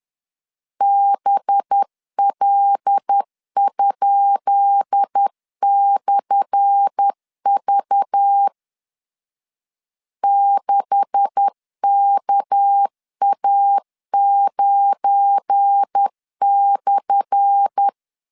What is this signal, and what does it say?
Morse code, heard as a keyed tone, about 10 wpm, 800 Hz. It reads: BL?/V 6KA9/
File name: LEZIONI / 10 SIGNS / 10B INT 10wpm 2844